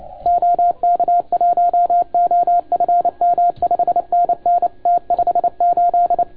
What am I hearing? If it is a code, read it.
Morse code, heard as a keyed tone, about 29 wpm, 690 Hz. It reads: OK1OFM 5NN T58
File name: Archiv / CUC287 / OK1FGD CUC 287